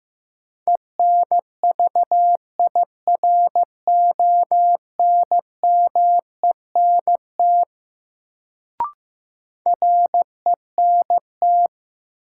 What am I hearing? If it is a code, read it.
Morse code, heard as a keyed tone, about 15 wpm, 700 Hz. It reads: ENVIRONMENT RENT
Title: Sentences from Top 1000 Words - Review Part 5 - New Word Drill - 15wpm